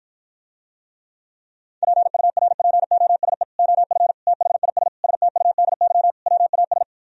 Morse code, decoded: PURPOSE OF THIS STUDY WAS

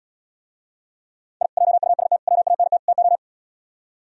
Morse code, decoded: I JUST WANT TO

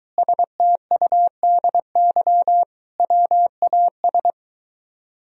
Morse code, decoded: STUDY WAS

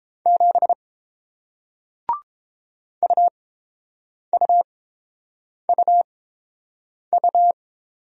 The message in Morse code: 7 UUUU

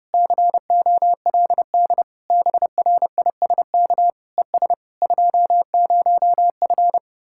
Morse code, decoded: COLD BRISK ES 20F

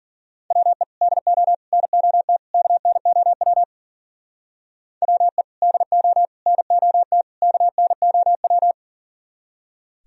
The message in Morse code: WE DO NOT KNOW WE DO NOT KNOW

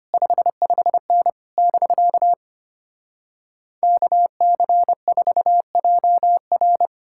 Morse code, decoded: 55N <BK> KC4JR

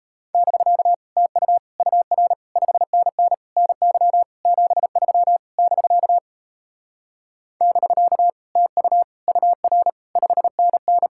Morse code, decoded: <BK> TU UR 5NN NY 73 <BK> <BK> TU UR 5NN

